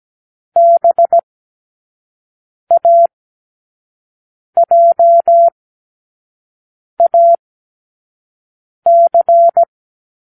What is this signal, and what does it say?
Morse code, heard as a keyed tone, about 17 wpm, 680 Hz. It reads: B A J A C
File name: k7qo / Lesson15(A-J)